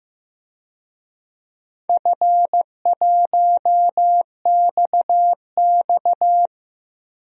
Morse code, heard as a keyed tone, about 15 wpm, 700 Hz. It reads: F1XX
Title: DXCC-100 Call Signs - 4 Characters - 15wpm